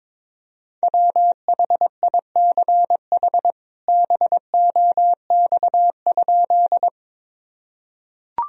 Morse code, heard as a keyed tone, about 21 wpm, 700 Hz. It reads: WHICH BOX?
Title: Sentences from Top 1000 Words - Review Part 2 - 22wpm